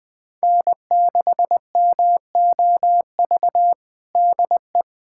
Morse code, heard as a keyed tone, about 20 wpm, 700 Hz. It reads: N6MOV DE
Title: USA Callsign DE Callsign - 5 Characters - 20wpm